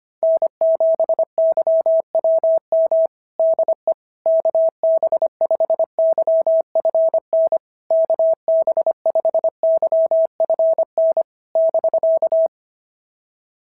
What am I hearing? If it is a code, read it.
Morse code, heard as a keyed tone, about 24 wpm, 650 Hz. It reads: N7YWM DE KB5YFN KB5YFN <BK>
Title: SOTA QSO Practice 25wpm